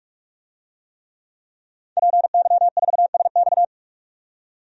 Morse code, heard as a keyed tone, about 46 wpm, 700 Hz. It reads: PY4SX
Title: DXCC-100 Call Signs - 5 Characters - 45wpm